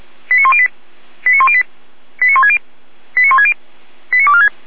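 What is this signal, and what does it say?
Five-tone selective call (ZVEI), heard as quick runs of five tones, tones of about 70 ms. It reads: 89108 89108 89160 89160 89246
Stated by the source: ZVEI five-tone selective call 89168 received on an SDR.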